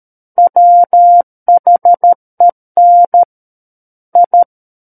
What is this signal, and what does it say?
Morse code, heard as a keyed tone, about 13 wpm, 700 Hz. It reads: WHEN I
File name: Book One / 04-Chapter 4 - 13 WPM